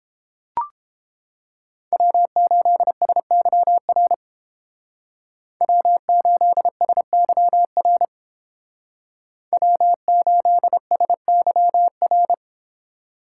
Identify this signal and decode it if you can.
Morse code, heard as a keyed tone, about 30 wpm, 700 Hz. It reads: W8SYR W8SYR W8SYR